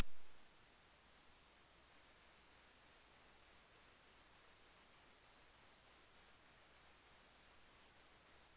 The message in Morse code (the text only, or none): none